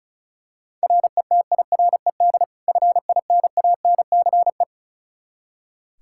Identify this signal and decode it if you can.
Morse code, heard as a keyed tone, about 36 wpm, 700 Hz. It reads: RETIRED FINANCE
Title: K1USN SST QSO Practice 35wpm